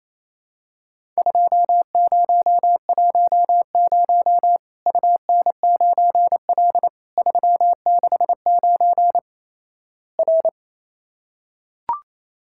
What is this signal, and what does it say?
Morse code, heard as a keyed tone, about 28 wpm, 700 Hz. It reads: 2010 UN9L 369 R
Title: Worked All Europe DX Contest - QTC Exchange - 28wpm